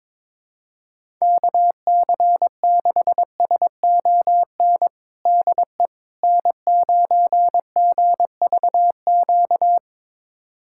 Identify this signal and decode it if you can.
Morse code, heard as a keyed tone, about 22 wpm, 700 Hz. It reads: KC6SON DE N9GVQ
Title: USA Callsign DE Callsign 22wpm